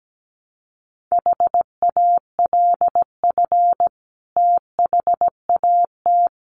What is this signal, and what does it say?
Morse code, heard as a keyed tone, about 17 wpm, 700 Hz. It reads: HALF THAT
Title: Sentences from Top 1400 Words 17wpm